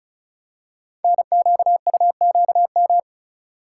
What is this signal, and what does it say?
Morse code, heard as a keyed tone, about 36 wpm, 700 Hz. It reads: NQUQM
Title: Code Groups - 5 Letters - 35wpm